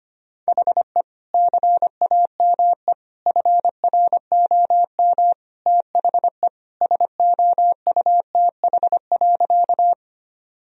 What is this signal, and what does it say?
Morse code, heard as a keyed tone, about 25 wpm, 700 Hz. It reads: HE CAME FROM THE SOUTH.
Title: Sentences from Top 1000 Words - Review Part 2 - 25wpm